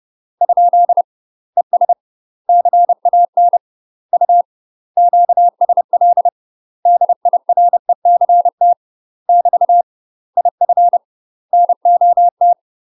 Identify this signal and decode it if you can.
Morse code, heard as a keyed tone, about 30 wpm, 700 Hz. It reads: ? ES CAN U QSL DIRECT <BT> IF NOT
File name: UK MASTER QSOs TRAINING / 30WPM / UK MASTER QSOs 0051